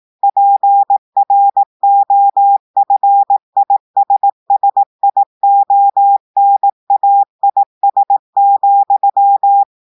Morse code, heard as a keyed tone, about 18 wpm, 800 Hz. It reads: PROFISSIONAIS,
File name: audios / portugais / portugais texte en clair 1 vitesse 18 mots minutes